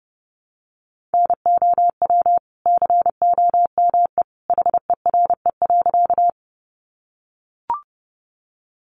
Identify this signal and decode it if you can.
Morse code, heard as a keyed tone, about 28 wpm, 700 Hz. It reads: NOW COME HERE.
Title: Sentences from Top 200 Words 30wpm